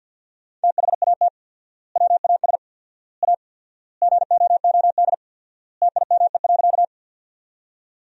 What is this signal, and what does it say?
Morse code, heard as a keyed tone, about 47 wpm, 700 Hz. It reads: THAT WAS A GOOD TIME.
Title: 4x Word-Spacing - Sentences from Top 200 Words - 50wpm